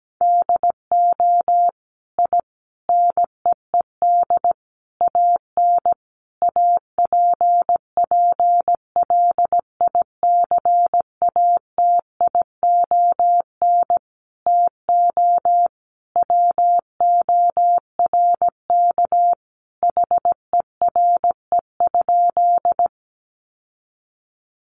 Morse code, heard as a keyed tone, about 17 wpm, 700 Hz. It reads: DO I NEED AN APPLICATION TO WORK HERE?
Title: Sentences from Top 1200 Words 17wpm